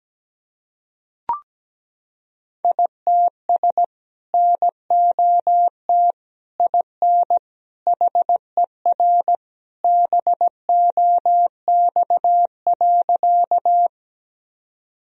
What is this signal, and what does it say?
Morse code, heard as a keyed tone, about 17 wpm, 700 Hz. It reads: ITS NOT IN HER BOX.
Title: Sentences from Top 1000 Words - Review Part 2 - 17wpm